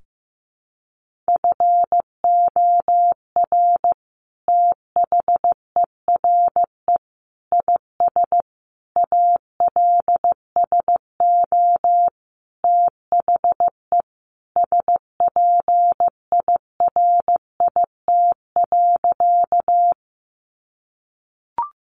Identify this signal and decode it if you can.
Morse code, heard as a keyed tone, about 15 wpm, 700 Hz. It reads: FOR THERE IS ALSO THE SPIRIT.
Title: Sentences from Top 1400 Words 15wpm